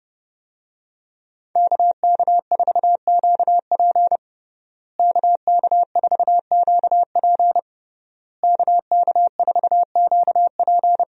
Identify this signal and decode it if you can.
Morse code, heard as a keyed tone, about 32 wpm, 700 Hz. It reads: KK4QP KK4QP KK4QP
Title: USA Call Signs - 5 Characters - Repeated 3x - 30wpm